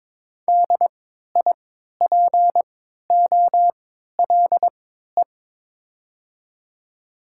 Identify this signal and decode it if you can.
Morse code, heard as a keyed tone, about 22 wpm, spaced as for 13 wpm, 700 Hz. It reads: DIPOLE
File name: Top 90 QSO Elements - 3x Character Spacing - 22wpm